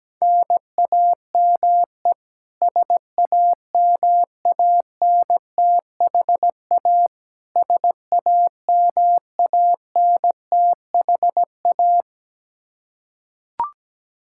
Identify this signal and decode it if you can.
Morse code, heard as a keyed tone, about 17 wpm, 700 Hz. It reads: NAME SAMANTHA SAMANTHA
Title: Practice QSO - Exchange 1 - Beginning - 17wpm